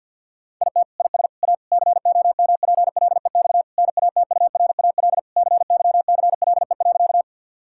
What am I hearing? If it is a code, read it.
Morse code, heard as a keyed tone, about 50 wpm, 700 Hz. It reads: IT IS A COMPLEX NATURAL CYCLE.
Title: Sentences from Top 1000 Words - Review Part 6 - 50wpm